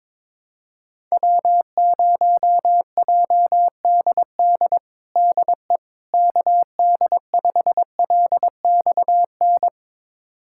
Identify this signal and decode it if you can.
Morse code, heard as a keyed tone, about 22 wpm, 700 Hz. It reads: W0JDD DE KD5LXN
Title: USA Callsign DE Callsign 22wpm